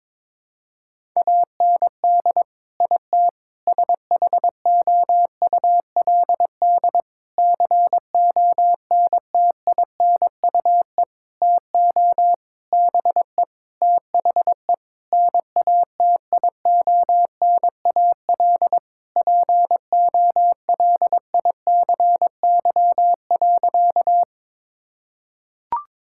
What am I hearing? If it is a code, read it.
Morse code, heard as a keyed tone, about 22 wpm, 700 Hz. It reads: AND IT SHOULD CONTINUE TO BE THE NATIONAL POLICY.
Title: Sentences from Top 1000 Words - Review Part 6 - 22wpm